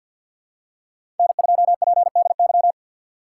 Morse code, heard as a keyed tone, about 50 wpm, 700 Hz. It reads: N2PDY
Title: USA Call Signs - 5 Characters - 6x Character Spacing - 50wpm